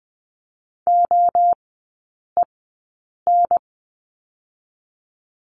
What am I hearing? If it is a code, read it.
Morse code, heard as a keyed tone, about 20 wpm, 700 Hz. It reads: O E N
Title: 023 Lesson 5e - Letter o - Three Characters 20wpm